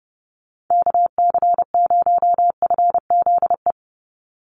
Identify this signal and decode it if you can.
Morse code, heard as a keyed tone, about 30 wpm, 700 Hz. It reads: KC0FZE
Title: USA Call Signs 30wpm